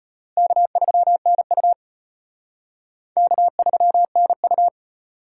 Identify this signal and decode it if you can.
Morse code, heard as a keyed tone, about 30 wpm, 700 Hz. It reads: K3NU K3NU
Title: Speed Racing - USA 1x2 Call Signs 25wpm